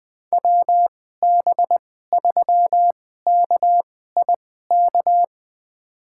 Morse code, heard as a keyed tone, about 21 wpm, spaced as for 15 wpm, 700 Hz. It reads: WB3KIK
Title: USA Call Signs - 6 Characters - 2x Character Spacing - 20wpm